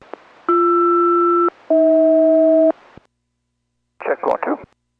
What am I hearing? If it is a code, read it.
Aircraft SELCAL, heard as two pairs of tones, with speech. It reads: BR-AH